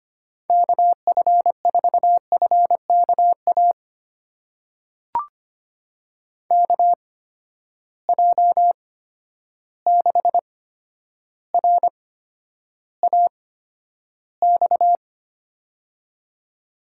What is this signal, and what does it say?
Morse code, heard as a keyed tone, about 25 wpm, spaced as for 11 wpm, 700 Hz. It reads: KF4FKA KJ6RAX
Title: USA Call Signs - 6 Characters - 8x Character Spacing - 25wpm